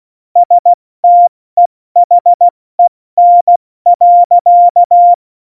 Morse code, heard as a keyed tone, about 16 wpm, spaced as for 14 wpm, 700 Hz. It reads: STEHEN.
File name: Morsekurs / Witze / Witz 26 Mai, 2020 70 F80